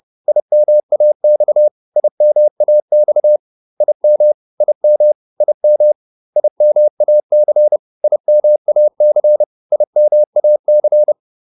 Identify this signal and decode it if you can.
Morse code, heard as a keyed tone, about 30 wpm, 600 Hz. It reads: IMAX IMAX IM IM IM IMAC IMAC IMAC